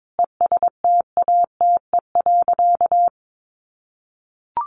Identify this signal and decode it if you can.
Morse code, heard as a keyed tone, about 22 wpm, 700 Hz. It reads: ESTATE.
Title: Sentences from Top 1200 Words 22wpm